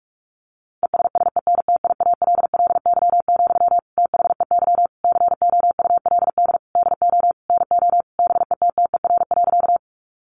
Morse code, heard as a keyed tone, about 44 wpm, 700 Hz. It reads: ESSENTIALLY, THEY COULD DO NO BETTER.
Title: Sentences from Top 1300 Words 45wpm